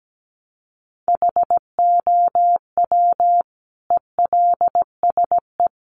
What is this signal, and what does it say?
Morse code, heard as a keyed tone, about 17 wpm, 700 Hz. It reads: HOW ELSE
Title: Sentences from Top 1300 Words 17wpm